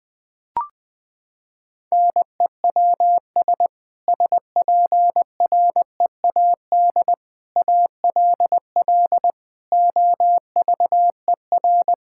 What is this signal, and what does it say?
Morse code, heard as a keyed tone, about 20 wpm, 700 Hz. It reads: NEWS SPREAD ALL OVER
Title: Sentences with 3-Word Phrases - 20wpm